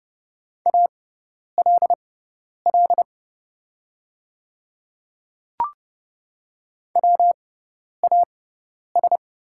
Morse code, heard as a keyed tone, about 29 wpm, spaced as for 10 wpm, 690 Hz. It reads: ALL WAS
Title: 6x Character Spacing - Top 100 Words - 30wpm